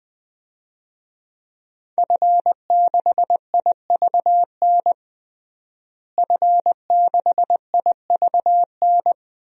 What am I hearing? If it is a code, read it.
Morse code, heard as a keyed tone, about 20 wpm, 700 Hz. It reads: F6IVN F6IVN
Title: DXCC-100 Call Signs - 5 Characters - Repeated 3x - 20wpm